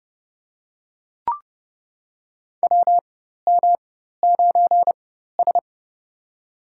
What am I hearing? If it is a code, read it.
Morse code, heard as a keyed tone, about 30 wpm, spaced as for 14 wpm, 700 Hz. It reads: WM9S
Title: USA Call Signs - 4 Characters - 4x Character Spacing - 30wpm